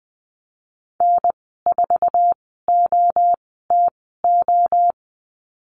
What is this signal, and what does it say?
Morse code, heard as a keyed tone, about 19 wpm, 700 Hz. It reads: N 4 O T O
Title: 048 Lesson 9g - Number 4 - Call Signs 20wpm